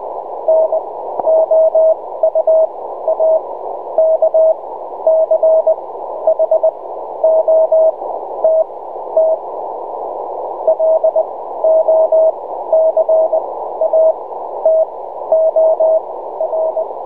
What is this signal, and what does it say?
Morse code, heard as a keyed tone, about 20 wpm, spaced as for 12 wpm, 640 Hz. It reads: NOUAKCHOTT LOCATOR
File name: Enregistrement QSO / 5t5sn b-2007